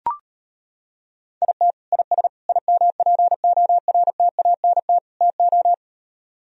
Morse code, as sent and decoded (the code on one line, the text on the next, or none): .. - / .. ... / .. -- .--. --- .-. - .- -. - / - ---
IT IS IMPORTANT TO